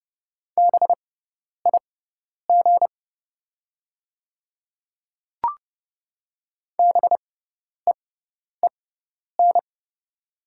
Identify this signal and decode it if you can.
Morse code, heard as a keyed tone, about 30 wpm, spaced as for 10 wpm, 700 Hz. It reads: BIG BEEN
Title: 6x Character Spacing - Top 500 Words - 30wpm